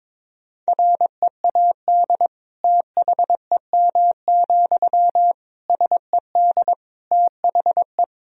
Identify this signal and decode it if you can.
Morse code, heard as a keyed tone, about 22 wpm, 700 Hz. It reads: READ THEM, SED THE
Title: Sentences from Top 400 Words and Abbreviations 22wpm